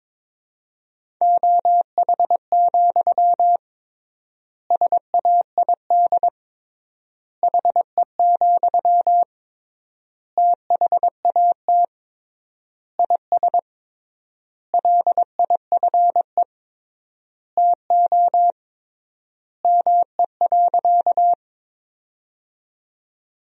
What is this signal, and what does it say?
Morse code, heard as a keyed tone, about 22 wpm, 700 Hz. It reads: OH, SAID HE, THAT IS LIFE TO ME.
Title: 3x Word-Spacing - Sentences from Top 500 Words - 22wpm